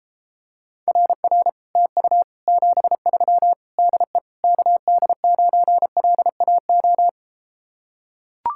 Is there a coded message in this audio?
Yes: Morse code from 0.9 to 7.1 s, heard as a keyed tone, about 33 wpm, 700 Hz. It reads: RR TU 73 DE KD9LAO